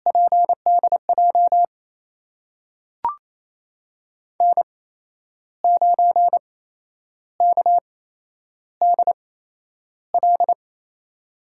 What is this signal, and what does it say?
Morse code, heard as a keyed tone, about 28 wpm, spaced as for 10 wpm, 700 Hz. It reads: PDJ N9KDL
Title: USA Call Signs - 5 Characters - 8x Character Spacing - 28wpm